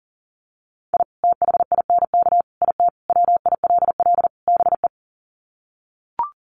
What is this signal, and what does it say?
Morse code, heard as a keyed tone, about 40 wpm, 700 Hz. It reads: I THINK IT WILL BE
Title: Sets of 5 Words Encore2 40wpm